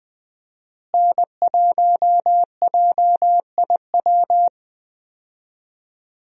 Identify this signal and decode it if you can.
Morse code, heard as a keyed tone, about 20 wpm, 700 Hz. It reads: N1JIW
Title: USA Call Signs - 5 Characters - Repeated 3x - 20wpm